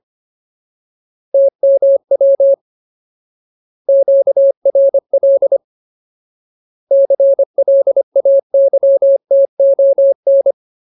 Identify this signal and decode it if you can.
Morse code, heard as a keyed tone, about 25 wpm, 550 Hz. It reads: TMW QRL CLAYTON
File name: Ultimate Word List / 25WPM Ultimate